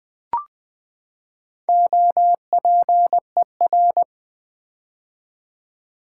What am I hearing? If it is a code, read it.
Morse code, heard as a keyed tone, about 20 wpm, 700 Hz. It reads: OPER
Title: Build A Word - 4 Letters At A Time - 8-Letter Words - 20wpm